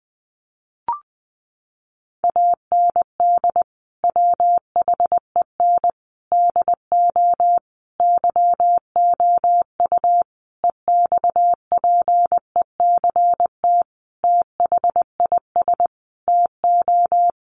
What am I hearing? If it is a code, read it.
Morse code, heard as a keyed tone, about 20 wpm, 700 Hz. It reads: AND WHEN DO YOU EXPECT THIS TO
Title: Sentences from Top 1100 Words 20wpm